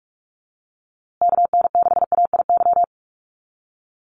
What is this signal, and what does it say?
Morse code, heard as a keyed tone, about 44 wpm, 700 Hz. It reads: KN6AIY
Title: USA Call Signs Repeated 2x 45wpm